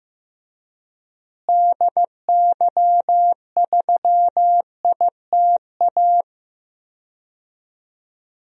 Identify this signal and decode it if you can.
Morse code, heard as a keyed tone, about 15 wpm, 700 Hz. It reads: DY3ITA
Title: DXCC-100 Call Signs - 6 Characters - Repeated 3x - 15wpm